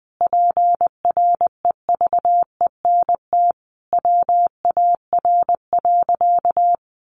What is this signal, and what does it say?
Morse code, heard as a keyed tone, about 19 wpm, 700 Hz. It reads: PREVENT WAR.
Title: Sentences from Top 1100 Words 20wpm